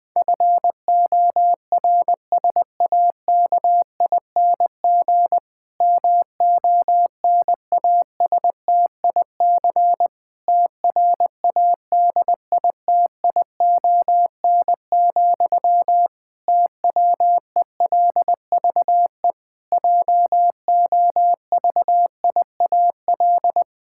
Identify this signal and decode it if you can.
Morse code, heard as a keyed tone, about 21 wpm, 700 Hz. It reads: FORSAKING MONASTIC TRADITION, TWELVE JOVIAL
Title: Pangram Challenge v2 - 20wpm